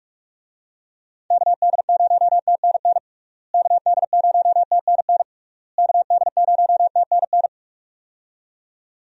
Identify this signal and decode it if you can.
Morse code, heard as a keyed tone, about 46 wpm, 700 Hz. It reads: KD0TNN KD0TNN KD0TNN
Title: USA Call Signs - 6 Characters - Repeated 3x - 45wpm